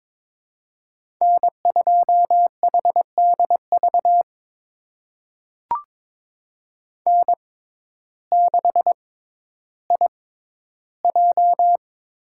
Morse code, heard as a keyed tone, about 22 wpm, spaced as for 12 wpm, 700 Hz. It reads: N2HDV N6IJ
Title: USA Call Signs - 5 Characters - 6x Character Spacing - 22wpm